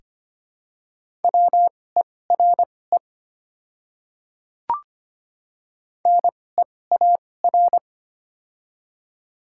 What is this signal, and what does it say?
Morse code, heard as a keyed tone, about 25 wpm, spaced as for 18 wpm, 700 Hz. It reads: WERE NEAR
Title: 2x Character Spacing - Top 200 Words - 25wpm